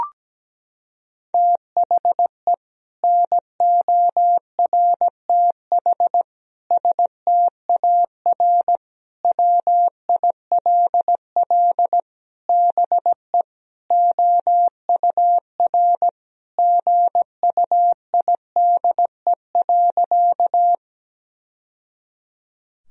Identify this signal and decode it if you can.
Morse code, heard as a keyed tone, about 17 wpm, 700 Hz. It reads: THE NORTH STAR WILL BE OUR GUIDE.
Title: Sentences from Top 1000 Words - Review Part 3 - 17wpm